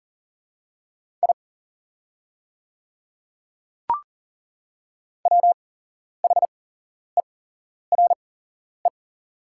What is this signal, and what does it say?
Morse code, heard as a keyed tone, about 40 wpm, spaced as for 11 wpm, 700 Hz. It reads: I WHERE
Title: 8x Character Spacing - Top 200 Words - 40wpm